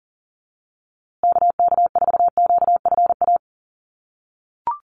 Morse code, heard as a keyed tone, about 40 wpm, 700 Hz. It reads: KK4QFA
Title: USA Call Signs Repeated 3x 40wpm